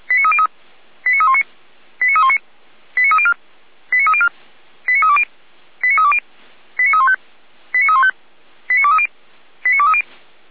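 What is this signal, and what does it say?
Five-tone selective call (ZVEI), heard as quick runs of five tones, tones of about 70 ms. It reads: 89319 89319 89304 89304 89320 89320 89315 89315 89230 89230
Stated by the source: Received on an SDR; ZVEI five-tone selective call.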